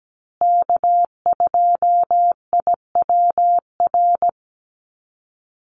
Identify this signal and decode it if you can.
Morse code, heard as a keyed tone, about 17 wpm, 700 Hz. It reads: K2IWR
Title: USA Call Signs Repeated 3x 17wpm